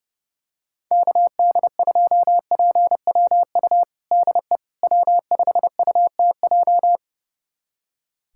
Morse code, heard as a keyed tone, about 32 wpm, 700 Hz. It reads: KD2PWU DE W5UTJ
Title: USA Callsign DE Callsign 30wpm